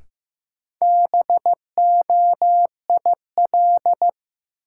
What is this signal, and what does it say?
Morse code, heard as a keyed tone, about 15 wpm, 700 Hz. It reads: BOIL
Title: Sets of 5 Words Encore - New Word Drill - 15wpm